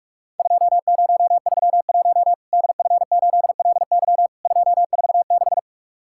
Morse code, heard as a keyed tone, about 43 wpm, 700 Hz. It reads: 2031 DF8LY 246
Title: Worked All Europe DX Contest - QTC Exchange - 45wpm